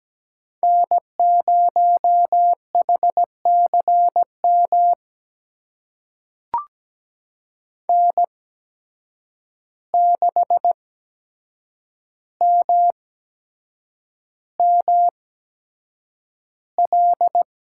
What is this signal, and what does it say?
Morse code, heard as a keyed tone, about 17 wpm, spaced as for 7 wpm, 700 Hz. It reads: N0HCM N6MML
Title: USA Call Signs - 5 Characters - 8x Character Spacing - 17wpm